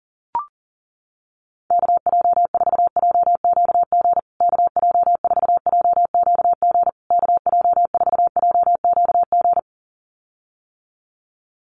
Morse code, heard as a keyed tone, about 40 wpm, 700 Hz. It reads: KJ4JQG KJ4JQG KJ4JQG
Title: USA Call Signs Repeated 3x 40wpm